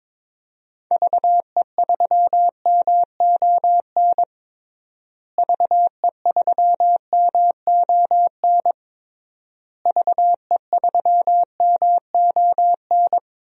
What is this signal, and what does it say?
Morse code, heard as a keyed tone, about 22 wpm, 700 Hz. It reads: VE3MON VE3MON VE3MON
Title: DXCC-100 Call Signs - 6 Characters - Repeated 3x - 22wpm